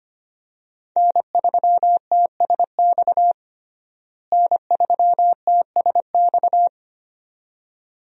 Morse code, heard as a keyed tone, about 25 wpm, 700 Hz. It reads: N3TSX N3TSX
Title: USA Call Signs - 5 Characters - Repeated 2x - 25wpm